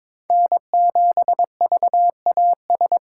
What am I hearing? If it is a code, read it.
Morse code, heard as a keyed tone, about 22 wpm, 700 Hz. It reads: N7VAS
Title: USA Call Signs - 2x Character Spacing - 22wpm